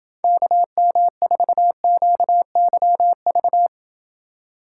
Morse code, heard as a keyed tone, about 28 wpm, 700 Hz. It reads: KM4QYV